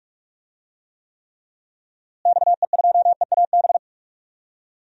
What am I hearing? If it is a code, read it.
Morse code, heard as a keyed tone, about 46 wpm, 700 Hz. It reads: XE2EAB
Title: DXCC-100 Call Signs - 6 Characters - Repeated 3x - 45wpm